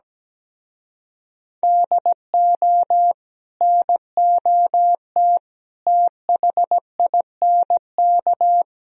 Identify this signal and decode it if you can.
Morse code, heard as a keyed tone, about 20 wpm, 700 Hz. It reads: DO NOT THINK
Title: Speed-Racing - Sets of 4 Words 17wpm